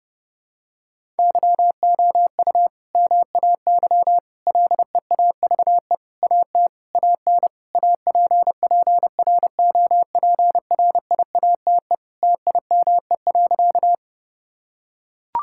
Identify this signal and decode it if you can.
Morse code, heard as a keyed tone, about 30 wpm, 700 Hz. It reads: YOU MAY LEAVE AT AN APPROPRIATE TIME.
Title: Sentences from Top 1000 Words - Review Part 5 - 30wpm